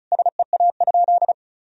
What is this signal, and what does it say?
Morse code, heard as a keyed tone, about 34 wpm, 700 Hz. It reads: SEA?
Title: Sentences from Top 1000 Words - Review Part 1 - 35wpm